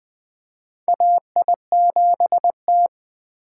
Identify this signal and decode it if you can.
Morse code, heard as a keyed tone, about 21 wpm, 700 Hz. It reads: AI7T
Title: USA Call Signs - 4 Characters - 4x Character Spacing - 20wpm